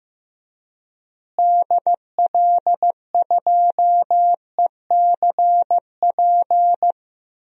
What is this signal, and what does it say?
Morse code, heard as a keyed tone, about 15 wpm, 700 Hz. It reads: DL2ECP
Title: DXCC-100 Call Signs - 6 Characters - Repeated 3x - 15wpm